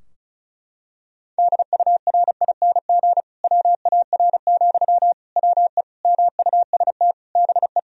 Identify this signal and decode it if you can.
Morse code, heard as a keyed tone, about 34 wpm, 700 Hz. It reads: DURING WAR, WE MUST BE
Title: Sentences from Top 1000 Words - Review Part 4 - 35wpm